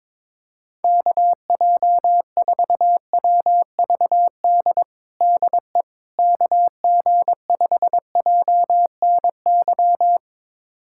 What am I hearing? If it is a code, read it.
Morse code, heard as a keyed tone, about 22 wpm, 700 Hz. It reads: KJ4WVD DE KG5JNY
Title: USA Callsign DE Callsign - 6 Characters - 22wpm